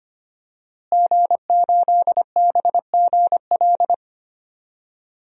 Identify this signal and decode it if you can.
Morse code, heard as a keyed tone, about 25 wpm, 700 Hz. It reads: G8BGL
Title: DXCC-100 Call Signs - 5 Characters - Repeated 3x - 25wpm